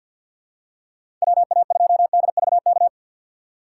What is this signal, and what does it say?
Morse code, heard as a keyed tone, about 52 wpm, 700 Hz. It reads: WA2DVK
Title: USA Call Signs - 6 Characters - 6x Character Spacing - 50wpm